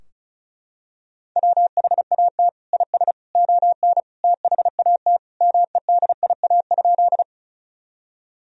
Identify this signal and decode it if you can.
Morse code, heard as a keyed tone, about 36 wpm, 700 Hz. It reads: WHAT IS ON THAT MEDIA?
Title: Sentences from Top 1000 Words - Review Part 4 - 35wpm